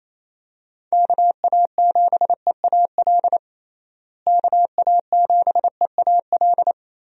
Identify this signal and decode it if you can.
Morse code, heard as a keyed tone, about 27 wpm, 700 Hz. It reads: KA7EAL KA7EAL
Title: USA Call Signs - 6 Characters - Repeated 3x - 28wpm